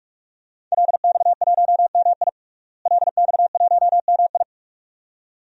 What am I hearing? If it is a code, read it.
Morse code, heard as a keyed tone, about 46 wpm, 700 Hz. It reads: LX1MI LX1MI
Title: DXCC-100 Call Signs - 5 Characters - Repeated 2x - 45wpm